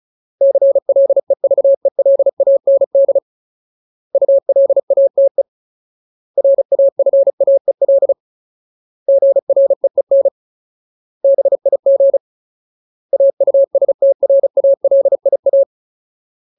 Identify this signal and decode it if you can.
Morse code, heard as a keyed tone, about 34 wpm, 550 Hz. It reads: CLEVELAND ULATE RAFAEL GREEN BIG AUSTRALIA
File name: Ultimate Word List / 35WPM Ultimate